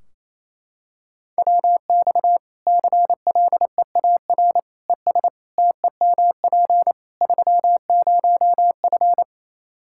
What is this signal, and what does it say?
Morse code, heard as a keyed tone, about 28 wpm, 700 Hz. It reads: WX CLEAR ES TEMP 30F
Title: Practice QSO - Exchange 2 - Info - 28wpm